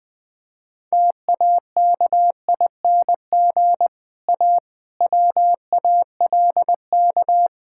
Morse code, heard as a keyed tone, about 20 wpm, 700 Hz. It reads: TAKING A WALK
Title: Difficult Sentences with Hints - 20wpm